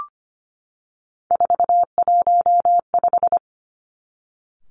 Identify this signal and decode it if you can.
Morse code, heard as a keyed tone, about 25 wpm, 700 Hz. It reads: 415
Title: Three Challenging Letter-Number Combinations 25wpm